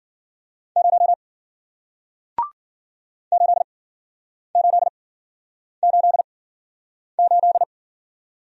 Morse code, heard as a keyed tone, about 53 wpm, spaced as for 9 wpm, 700 Hz. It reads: 0 8888